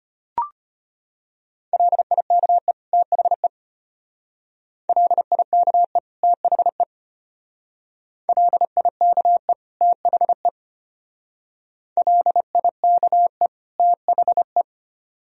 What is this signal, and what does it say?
Morse code, heard as a keyed tone, about 33 wpm, 700 Hz. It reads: LIKE THE LIKE THE LIKE THE LIKE THE